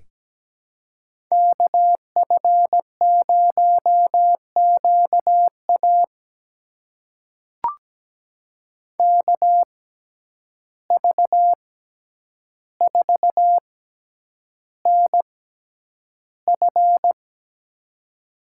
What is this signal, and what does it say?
Morse code, heard as a keyed tone, about 17 wpm, spaced as for 9 wpm, 700 Hz. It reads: KF0QA KV4NF
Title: USA Call Signs - 5 Characters - 6x Character Spacing - 17wpm